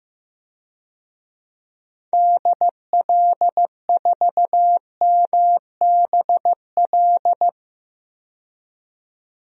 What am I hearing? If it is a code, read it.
Morse code, heard as a keyed tone, about 15 wpm, 700 Hz. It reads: DL4MBL
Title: DXCC-100 Call Signs - 6 Characters - 15wpm